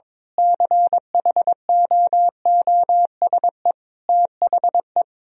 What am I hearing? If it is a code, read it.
Morse code, heard as a keyed tone, about 22 wpm, 700 Hz. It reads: CHOOSE THE
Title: Sentences from Top 1000 Words - Review Part 4 - 22wpm